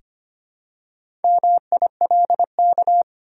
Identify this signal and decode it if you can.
Morse code, heard as a keyed tone, about 25 wpm, 690 Hz. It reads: MILK
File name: practice / 25wpm / Easy Words with Hints - 25wpm